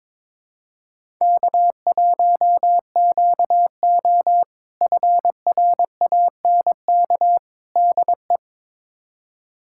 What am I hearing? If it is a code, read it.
Morse code, heard as a keyed tone, about 22 wpm, 700 Hz. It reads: K1QO FRANK DE
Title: K1USN SST QSO Practice - 22wpm